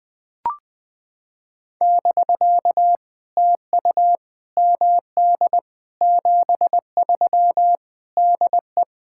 Morse code, heard as a keyed tone, about 20 wpm, 700 Hz. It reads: <BK> TU MD 73 DE
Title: POTA QSO Practice - 20wpm